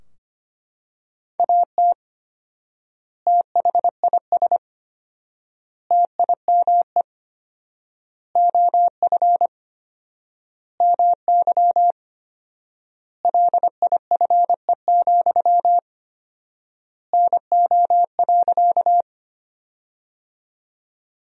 Morse code, heard as a keyed tone, about 25 wpm, 700 Hz. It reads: AT THIS TIME OF MY LIFE, NO.
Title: 4x Word-Spacing - Sentences from Top 300 Words - 25wpm